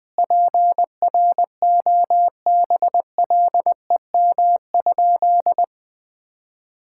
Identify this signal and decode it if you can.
Morse code, heard as a keyed tone, about 20 wpm, 700 Hz. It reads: PROBLEM?
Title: Sentences from Top 1000 Words - Review Part 6 - 20wpm